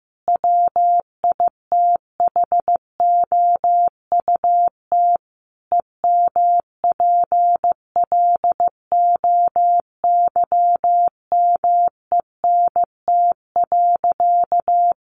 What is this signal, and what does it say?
Morse code, heard as a keyed tone, about 15 wpm, 700 Hz. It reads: WITHOUT EMPLOYMENT.
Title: Sentences from Top 1500 Words 15wpm